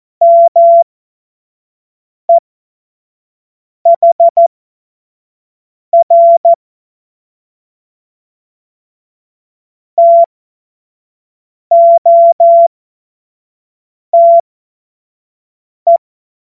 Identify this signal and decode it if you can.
Morse code, heard as a keyed tone, about 14 wpm, spaced as for 5 wpm, 680 Hz. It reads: MEHR TOTE
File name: Morsekurs / Witze / Witz 06. Juli 2021, 25 F70